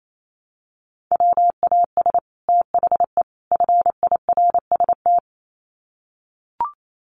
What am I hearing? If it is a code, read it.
Morse code, heard as a keyed tone, about 28 wpm, 700 Hz. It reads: WAS THE FIRST
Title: Sets of 3 Words 28wpm